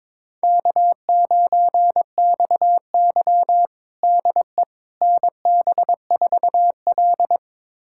Morse code, heard as a keyed tone, about 21 wpm, 700 Hz. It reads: K9XY DE NB4L
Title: USA Callsign DE Callsign - 4 Characters - 22wpm